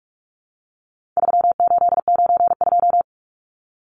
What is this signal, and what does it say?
Morse code, heard as a keyed tone, about 44 wpm, 700 Hz. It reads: 3892
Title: Numbers - 4-Digit 45wpm